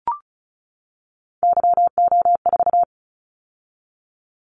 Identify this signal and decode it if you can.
Morse code, heard as a keyed tone, about 36 wpm, 700 Hz. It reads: YO4
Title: Callsign Prefixes for Top 25 DXCC - 35wpm